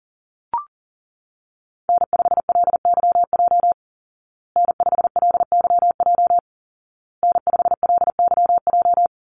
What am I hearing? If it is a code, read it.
Morse code, heard as a keyed tone, about 40 wpm, 700 Hz. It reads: N5LYJ N5LYJ N5LYJ
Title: Call Signs Repeated 2x - 40wpm